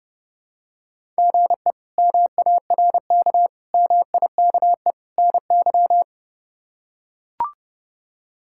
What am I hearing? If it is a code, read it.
Morse code, heard as a keyed tone, about 30 wpm, 700 Hz. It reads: GE MARK MIKE NY
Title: K1USN SST QSO Practice - 30wpm